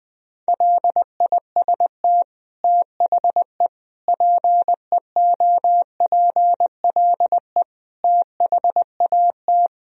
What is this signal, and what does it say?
Morse code, heard as a keyed tone, about 20 wpm, 700 Hz. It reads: LIST THE PEOPLE THAT
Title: Sentences from Top 1000 Words - Review Part 2 - 20wpm